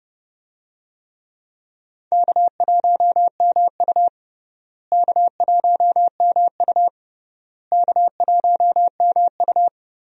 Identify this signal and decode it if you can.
Morse code, heard as a keyed tone, about 30 wpm, 700 Hz. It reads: K1MU K1MU K1MU
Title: USA Call Signs - 4 Characters - Repeated 3x - 30wpm